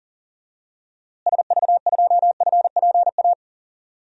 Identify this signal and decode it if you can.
Morse code, heard as a keyed tone, about 40 wpm, 700 Hz. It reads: SV2FPA
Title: DXCC-100 Call Signs - 6 Characters - Repeated 3x - 40wpm